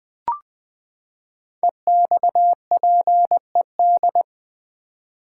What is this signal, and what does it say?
Morse code, heard as a keyed tone, about 20 wpm, 700 Hz. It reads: EXPED
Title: Build A Word - 5 Letters At A Time - 10-Letter Words - 20wpm